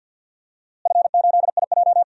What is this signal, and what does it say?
Morse code, heard as a keyed tone, about 45 wpm, 700 Hz. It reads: F8IJ
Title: DXCC-100 Call Signs - 4 Characters - Repeated 3x - 50wpm